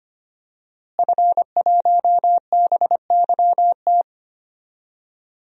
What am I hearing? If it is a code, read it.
Morse code, heard as a keyed tone, about 25 wpm, 700 Hz. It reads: F1BYT
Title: DXCC-100 Call Signs - 5 Characters - Repeated 3x - 25wpm